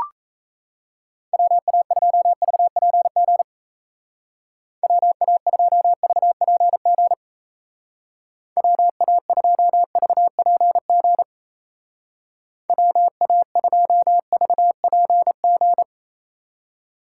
Morse code, heard as a keyed tone, about 34 wpm, 700 Hz. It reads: WA2VPG WA2VPG WA2VPG WA2VPG
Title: Speed Racing - USA 2x3 Call Signs 28wpm